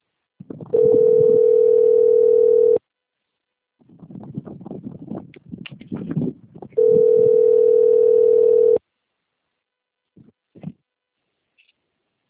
Ringback tone (North America), 440 + 480 Hz.